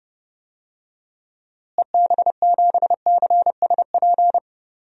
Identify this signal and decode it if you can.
Morse code, heard as a keyed tone, about 30 wpm, 700 Hz. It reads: EB7CSP
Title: DXCC-100 Call Signs - 6 Characters - Repeated 3x - 30wpm